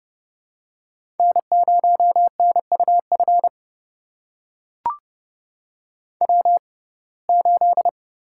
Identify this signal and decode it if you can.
Morse code, heard as a keyed tone, about 30 wpm, spaced as for 22 wpm, 700 Hz. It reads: N0NUF W8
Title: USA Call Signs - 5 Characters - 6x Character Spacing - 30wpm